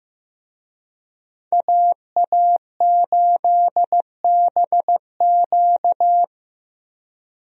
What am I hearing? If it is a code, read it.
Morse code, heard as a keyed tone, about 16 wpm, 700 Hz. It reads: AA8BQ
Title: USA Call Signs - 5 Characters - Repeated 3x - 15wpm